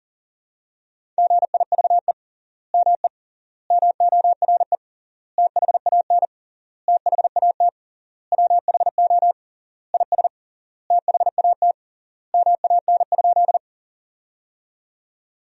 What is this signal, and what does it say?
Morse code, heard as a keyed tone, about 40 wpm, 700 Hz. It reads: GIVE ME MORE THAN THAT WHO IS THAT MAN?
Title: 3x Word-Spacing - Sentences from Top 200 Words - 40wpm